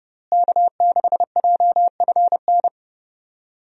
Morse code, heard as a keyed tone, about 29 wpm, 700 Hz. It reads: K6JFN
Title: USA Call Signs - 5 Characters - Repeated 2x - 30wpm